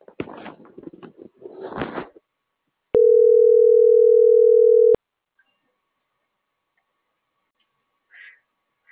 Ringback tone (North America), 440 + 480 Hz.